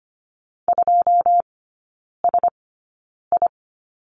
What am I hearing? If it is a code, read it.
Morse code, heard as a keyed tone, about 25 wpm, spaced as for 9 wpm, 700 Hz. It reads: 2SI
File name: Lesson 17e - Letter u - Three Characters 25wpm